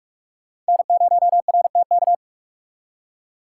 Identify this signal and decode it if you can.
Morse code, heard as a keyed tone, about 59 wpm, 700 Hz. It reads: N0RTK